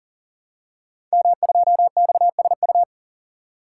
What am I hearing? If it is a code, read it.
Morse code, heard as a keyed tone, about 39 wpm, 700 Hz. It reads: M2XSU
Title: Code Groups - 5 Letter:Numbers - 40wpm